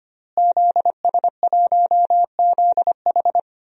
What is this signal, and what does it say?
Morse code, heard as a keyed tone, about 25 wpm, 700 Hz. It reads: ZS1ZH
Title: DXCC-100 Call Signs - 5 Characters - Repeated 2x - 25wpm